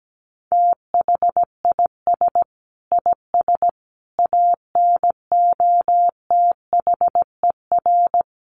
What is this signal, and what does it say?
Morse code, heard as a keyed tone, about 17 wpm, 700 Hz. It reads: THIS IS ANOTHER
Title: Sentences from Top 600 Words 17wpm